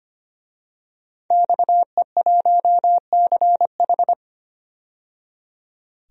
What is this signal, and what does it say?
Morse code, heard as a keyed tone, about 25 wpm, 700 Hz. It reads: XE1CH
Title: DXCC-100 Call Signs - 5 Characters - 25wpm